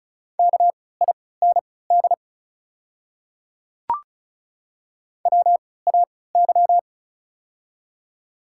Morse code, heard as a keyed tone, about 36 wpm, spaced as for 20 wpm, 700 Hz. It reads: KIND WAY
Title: 3x Character Spacing - Top 200 Words - 35wpm